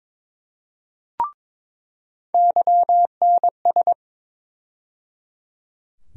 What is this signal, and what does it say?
Morse code, heard as a keyed tone, about 22 wpm, 700 Hz. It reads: YNS
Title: USA Call Sign Postfixes - 3 Letters 22wpm